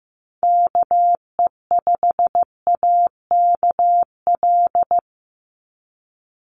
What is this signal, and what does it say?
Morse code, heard as a keyed tone, about 15 wpm, 700 Hz. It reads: KE5AKL
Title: NA SOTA Activator Callsigns 15wpm